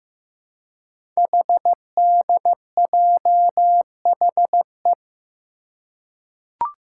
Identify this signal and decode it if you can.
Morse code, heard as a keyed tone, about 15 wpm, 700 Hz. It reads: HDJHE
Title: Code Groups - 5 Letters - 15wpm